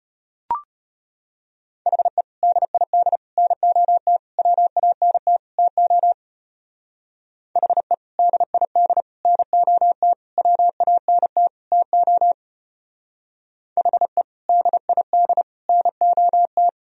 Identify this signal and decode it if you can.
Morse code, heard as a keyed tone, about 31 wpm, 700 Hz. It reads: HE DID NOT WANT TO HE DID NOT WANT TO HE DID NOT
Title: Speed-Racing - Sets of 5 Words 25wpm